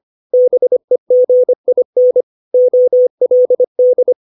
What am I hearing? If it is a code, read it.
Morse code, heard as a keyed tone, about 25 wpm, 500 Hz. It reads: BEGIN OLD
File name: 632 words / 632 Words 25WPM